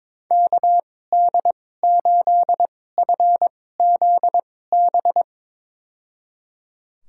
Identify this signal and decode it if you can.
Morse code, heard as a keyed tone, about 22 wpm, spaced as for 16 wpm, 700 Hz. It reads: KD8FZB
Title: USA Call Signs - 6 Characters - 2x Character Spacing - 22wpm